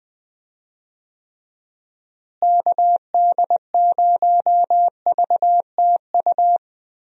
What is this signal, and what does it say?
Morse code, heard as a keyed tone, about 20 wpm, 700 Hz. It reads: KD0VTU
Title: USA Call Signs - 6 Characters - Repeated 3x - 20wpm